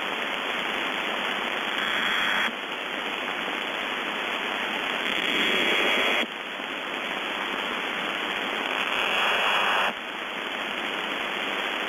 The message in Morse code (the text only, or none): none